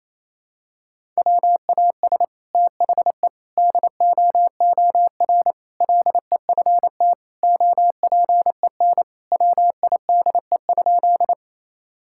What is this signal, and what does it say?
Morse code, heard as a keyed tone, about 28 wpm, 700 Hz. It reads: WAS THE DOOR LEFT OPEN WIDE?
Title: Sentences from Top 1000 Words - Review Part 6 - 28wpm